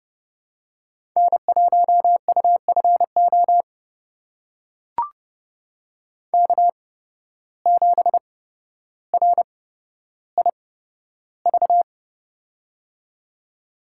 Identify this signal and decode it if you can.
Morse code, heard as a keyed tone, about 30 wpm, spaced as for 13 wpm, 700 Hz. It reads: N1UFO K7RIV
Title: USA Call Signs - 8x Character Spacing - 30wpm